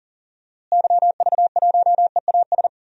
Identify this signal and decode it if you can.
Morse code, heard as a keyed tone, about 38 wpm, 700 Hz. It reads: YV1EAS
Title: DXCC-100 Call Signs - 6 Characters - Repeated 2x - 40wpm